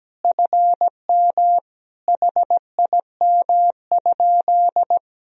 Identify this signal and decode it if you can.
Morse code, heard as a keyed tone, about 17 wpm, 700 Hz. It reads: FM HIM?
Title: Sentences from Top 300 Words and Abbreviations 17wpm